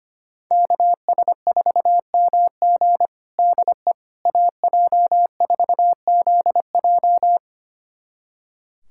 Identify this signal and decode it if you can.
Morse code, heard as a keyed tone, about 25 wpm, 700 Hz. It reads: KS4MG DE AJ4ZJ
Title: USA Callsign DE Callsign - 5 Characters - 25wpm